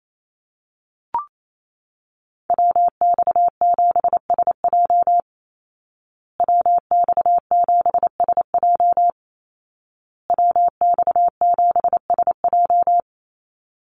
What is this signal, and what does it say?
Morse code, heard as a keyed tone, about 28 wpm, 700 Hz. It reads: WX7SJ WX7SJ WX7SJ
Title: Call Signs Repeated 2x - 28wpm